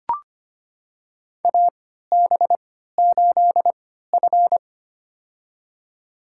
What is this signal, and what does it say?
Morse code, heard as a keyed tone, about 25 wpm, spaced as for 14 wpm, 700 Hz. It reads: AB8F
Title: USA Call Signs - 4 Characters - 3x Character Spacing - 25wpm